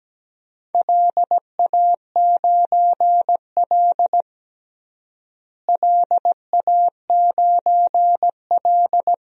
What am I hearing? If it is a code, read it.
Morse code, heard as a keyed tone, about 17 wpm, 700 Hz. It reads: LA9L LA9L
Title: DXCC-100 Call Signs - 4 Characters - Repeated 3x - 17wpm